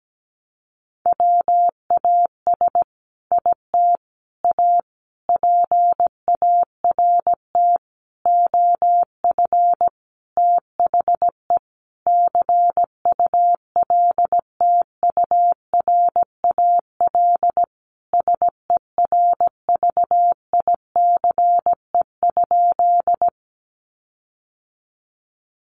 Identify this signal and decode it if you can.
Morse code, heard as a keyed tone, about 17 wpm, 700 Hz. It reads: WAS IT A PART OF THE CULTURAL SERVICE?
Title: Sentences from Top 1500 Words 17wpm